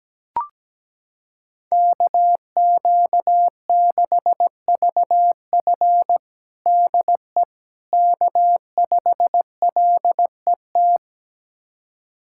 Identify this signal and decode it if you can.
Morse code, heard as a keyed tone, about 17 wpm, 700 Hz. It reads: KQ6VF DE K5LET
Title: USA Callsign DE Callsign - 5 Characters - 17wpm